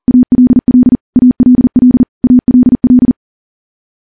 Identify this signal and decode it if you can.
Morse code, heard as a keyed tone, about 39 wpm, 250 Hz. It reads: ALL ALL ALL
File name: morse / morse-training-40WPM